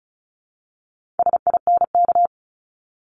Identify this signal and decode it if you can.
Morse code, heard as a keyed tone, about 36 wpm, 700 Hz. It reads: SINK
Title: Multi-letter Phonemes Repeated 3x 35wpm